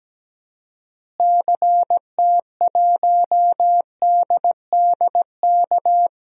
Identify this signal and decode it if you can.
Morse code, heard as a keyed tone, about 17 wpm, 700 Hz. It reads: CT1DDK
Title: DXCC-100 Call Signs - 6 Characters - Repeated 3x - 17wpm